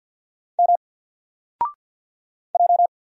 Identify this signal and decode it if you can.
Morse code, heard as a keyed tone, about 50 wpm, 700 Hz. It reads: M J
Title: Single Letter-Number Alt 50wpm